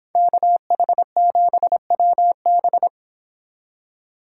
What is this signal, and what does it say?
Morse code, heard as a keyed tone, about 29 wpm, 700 Hz. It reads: KH7WB